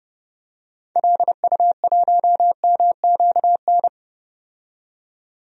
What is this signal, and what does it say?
Morse code, heard as a keyed tone, about 30 wpm, 700 Hz. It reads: LU1MQN
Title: DXCC-100 Call Signs - 6 Characters - 30wpm